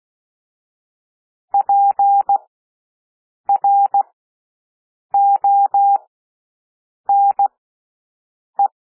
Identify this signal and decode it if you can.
Morse code, heard as a keyed tone, about 16 wpm, spaced as for 6 wpm, 800 Hz. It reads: PRONE
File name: cwmp3 / W1 SOUND WORDS 16 wpm